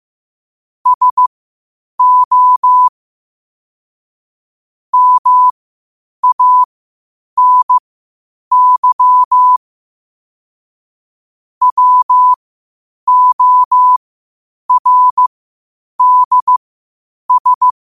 Morse code, heard as a keyed tone, about 15 wpm, spaced as for 8 wpm, 1000 Hz. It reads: SO MANY WORDS
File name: qotdfiles / 8wpm